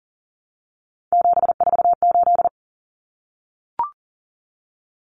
Morse code, heard as a keyed tone, about 40 wpm, 700 Hz. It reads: 748
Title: Common Cut Numbers - 3-Digit 40wpm